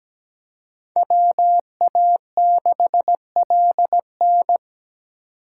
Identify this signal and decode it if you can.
Morse code, heard as a keyed tone, about 20 wpm, 700 Hz. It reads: WA6LN
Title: Speed Racing - USA Call Signs 17wpm